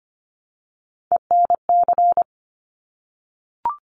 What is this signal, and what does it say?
Morse code, heard as a keyed tone, about 25 wpm, 700 Hz. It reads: ENC
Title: Top Three Letter Combinations in Words 25wpm